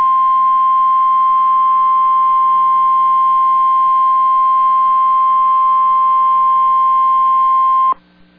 Emergency Alert System (SAME) tones (the 1050 Hz attention tone). It is heard from 1.1 to 7.7 s.